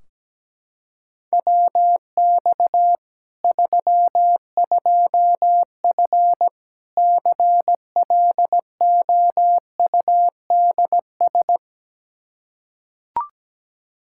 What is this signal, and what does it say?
Morse code, heard as a keyed tone, about 17 wpm, 700 Hz. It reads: WX 32F CLOUDS
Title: Practice QSO - Exchange 2 - Info - 17wpm